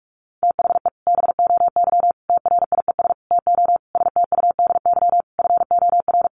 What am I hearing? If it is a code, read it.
Morse code, heard as a keyed tone, about 44 wpm, 700 Hz. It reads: THE BOY TRIES TO STUDY FOR
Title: Additional Easy Phrases 45wpm